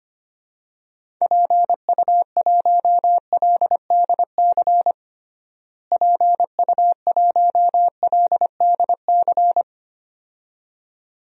Morse code, heard as a keyed tone, about 25 wpm, 700 Hz. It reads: PU1LDC PU1LDC
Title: DXCC-100 Call Signs - 6 Characters - Repeated 2x - 25wpm